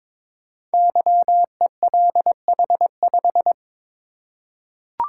Morse code, heard as a keyed tone, about 22 wpm, 700 Hz. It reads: YELH5
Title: Code Groups - 5 Letter:Numbers - 22wpm